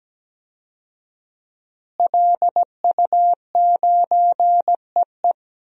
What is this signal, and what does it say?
Morse code, heard as a keyed tone, about 17 wpm, 700 Hz. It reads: LU9EE
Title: DXCC-100 Call Signs - 5 Characters - Repeated 2x - 17wpm